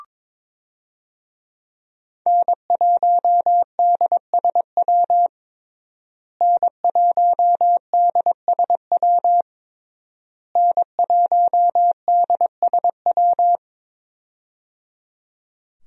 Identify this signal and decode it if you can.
Morse code, heard as a keyed tone, about 21 wpm, 700 Hz. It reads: N1DSW N1DSW N1DSW
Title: USA Call Signs - 5 Characters - Repeated 3x - 22wpm